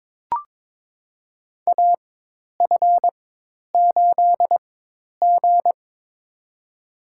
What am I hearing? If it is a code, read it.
Morse code, heard as a keyed tone, about 22 wpm, spaced as for 10 wpm, 700 Hz. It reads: AF8G
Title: USA Call Signs - 4 Characters - 4x Character Spacing - 22wpm